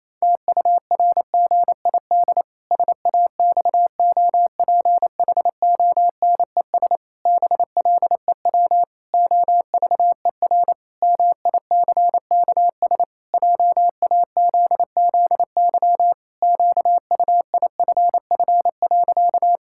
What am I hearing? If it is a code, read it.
Morse code, heard as a keyed tone, about 28 wpm, 700 Hz. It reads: TURGID SAXOPHONES BLEW OVER MICKS JAZZY QUIFF.
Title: Pangram Challenge v2 - 28wpm